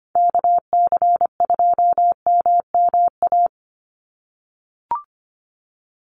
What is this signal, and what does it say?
Morse code, heard as a keyed tone, about 25 wpm, 700 Hz. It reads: KC2MMA
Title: USA Call Signs 25wpm